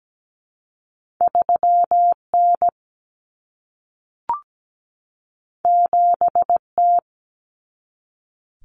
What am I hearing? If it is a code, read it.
Morse code, heard as a keyed tone, about 17 wpm, 700 Hz. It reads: 3N 7T
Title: Common Cut Numbers - 2-Digit 17wpm